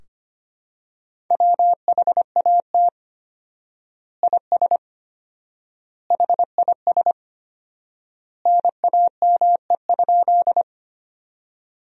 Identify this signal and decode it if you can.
Morse code, heard as a keyed tone, about 25 wpm, 700 Hz. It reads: WHAT IS HIS NAME?
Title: 4x Word-Spacing - Sentences from Top 200 Words - 25wpm